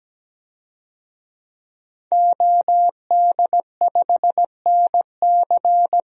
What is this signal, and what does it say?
Morse code, heard as a keyed tone, about 17 wpm, 700 Hz. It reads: OD5NC
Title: DXCC-100 Call Signs - 5 Characters - Repeated 2x - 17wpm